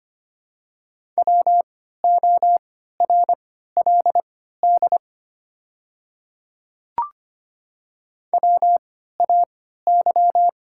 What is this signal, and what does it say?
Morse code, heard as a keyed tone, about 25 wpm, spaced as for 14 wpm, 700 Hz. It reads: WORLD WAY
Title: 3x Character Spacing - Top 300 Words - 25wpm